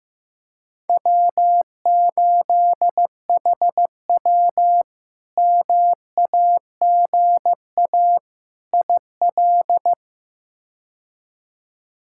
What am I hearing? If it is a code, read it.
Morse code, heard as a keyed tone, about 15 wpm, 700 Hz. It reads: W8HW MAGA IL
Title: K1USN SST QSO Practice - 15wpm